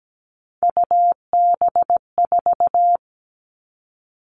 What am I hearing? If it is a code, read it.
Morse code, heard as a keyed tone, about 17 wpm, 700 Hz. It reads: UB4
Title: Three Challenging Letter-Number Combinations 17wpm